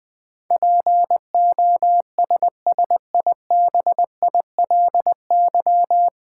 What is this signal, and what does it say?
Morse code, heard as a keyed tone, about 20 wpm, 700 Hz. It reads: POSSIBILY
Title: Sentences from Top 1300 Words - New Word Drill Alt - 20wpm